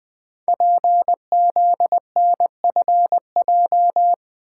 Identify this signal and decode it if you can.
Morse code, heard as a keyed tone, about 20 wpm, 700 Hz. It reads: PZNFJ
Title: Code Groups - 5 Letters - 20wpm